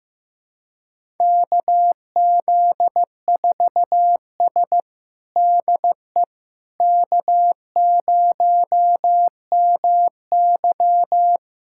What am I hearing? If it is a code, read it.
Morse code, heard as a keyed tone, about 15 wpm, 700 Hz. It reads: KZ4S DE K0MY
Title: USA Callsign DE Callsign - 4 Characters - 15wpm